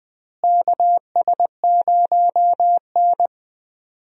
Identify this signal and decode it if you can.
Morse code, heard as a keyed tone, about 20 wpm, 700 Hz. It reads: KS0N
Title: USA Call Signs - 4 Characters - Repeated 2x - 20wpm